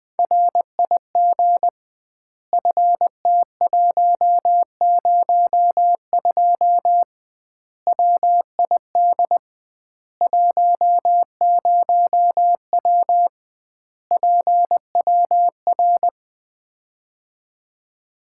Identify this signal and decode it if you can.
Morse code, heard as a keyed tone, about 20 wpm, 700 Hz. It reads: RIG FT102 WID 10W PWR
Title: Common QSO Phrases - 2x Word-Spacing - 20wpm